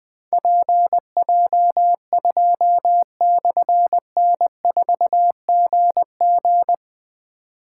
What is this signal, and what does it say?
Morse code, heard as a keyed tone, about 21 wpm, 700 Hz. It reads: PJ2/N4GG
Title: Difficult Call Signs 20wpm